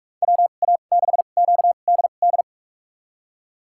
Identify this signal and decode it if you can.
Morse code, heard as a keyed tone, about 46 wpm, spaced as for 32 wpm, 700 Hz. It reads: WA6QDD
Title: USA Call Signs - 6 Characters - 2x Character Spacing - 45wpm